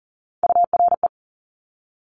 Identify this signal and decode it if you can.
Morse code, heard as a keyed tone, about 39 wpm, 700 Hz. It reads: URE
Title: Top Three Letter Combinations in Words 40wpm